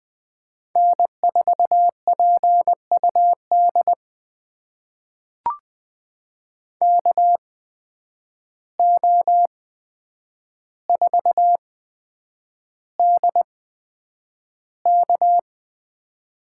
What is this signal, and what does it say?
Morse code, heard as a keyed tone, about 20 wpm, spaced as for 9 wpm, 700 Hz. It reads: N4PUD KO4DK
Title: USA Call Signs - 5 Characters - 8x Character Spacing - 20wpm